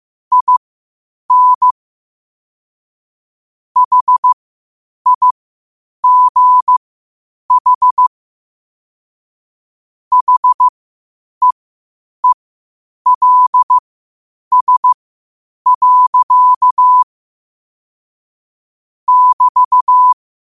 Morse code, heard as a keyed tone, about 15 wpm, spaced as for 8 wpm, 1000 Hz. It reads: IN HIGH HEELS. <BT>
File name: qotdfiles / 8wpm